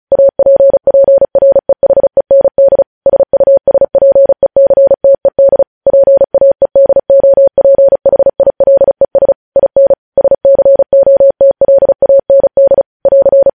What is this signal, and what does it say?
Morse code, heard as a keyed tone, about 36 wpm, 550 Hz. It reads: APPREHEND SUSPECTED PAEDOPHILES IN SCOTLAND <AR>
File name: podcast / News headlines-1x-35x35